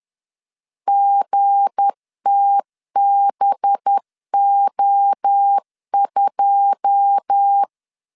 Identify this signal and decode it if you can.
Morse code, heard as a keyed tone, about 10 wpm, 800 Hz. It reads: GTBO2
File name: LEZIONI / 9 2 10wpm 2736